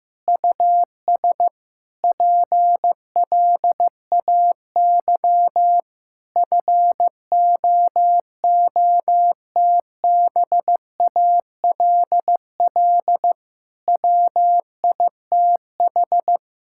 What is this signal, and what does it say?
Morse code, heard as a keyed tone, about 15 wpm, 700 Hz. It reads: US PLAY FOOTBALL WITH
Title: Sentences from Top 1600 Words 15wpm